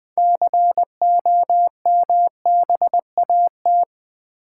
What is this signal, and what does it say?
Morse code, heard as a keyed tone, about 20 wpm, 700 Hz. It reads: COMBAT
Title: Build A Word - 3 Letters At A Time - 6-Letter Words - 20wpm